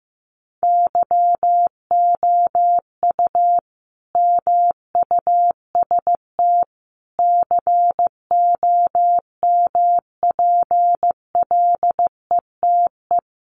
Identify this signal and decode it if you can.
Morse code, heard as a keyed tone, about 15 wpm, 700 Hz. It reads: YOU MUST COMPLETE
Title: Sentences from Top 1400 Words 15wpm